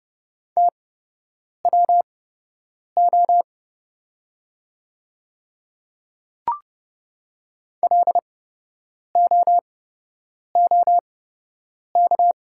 Morse code, heard as a keyed tone, about 30 wpm, spaced as for 8 wpm, 700 Hz. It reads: TWO LOOK
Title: 8x Character Spacing - Top 100 Words - 30wpm